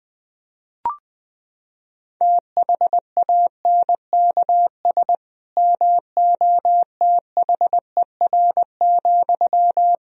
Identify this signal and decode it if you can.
Morse code, heard as a keyed tone, about 21 wpm, 700 Hz. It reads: THANKS MOTHER,
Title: Sentences from Top 1000 Words - Review Part 3 - 20wpm